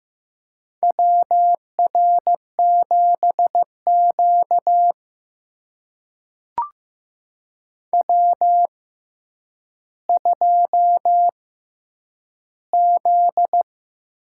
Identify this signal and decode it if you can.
Morse code, heard as a keyed tone, about 15 wpm, spaced as for 9 wpm, 700 Hz. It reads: WR7Q W2Z
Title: USA Call Signs - 4 Characters - 6x Character Spacing - 15wpm